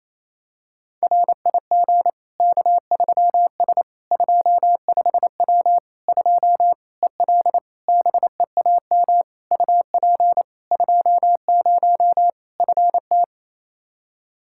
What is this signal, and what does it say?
Morse code, heard as a keyed tone, about 28 wpm, 700 Hz. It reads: RIG K3S 25W 2 EL BEAM UP 20 FT
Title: Practice QSO - Exchange 2 - Info - 28wpm